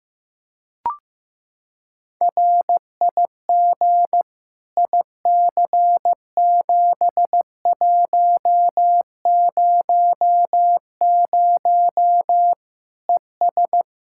Morse code, heard as a keyed tone, about 16 wpm, 700 Hz. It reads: RIG IC7100 ES
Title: Practice QSO - Exchange 2 - Info - 15wpm